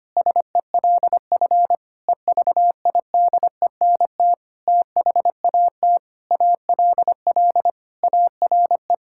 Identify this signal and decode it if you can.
Morse code, heard as a keyed tone, about 25 wpm, 700 Hz. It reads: SELF EVIDENT THAT ALL ARE
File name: practice / 25wpm / Sentences with 9-Word Phrases - 25wpm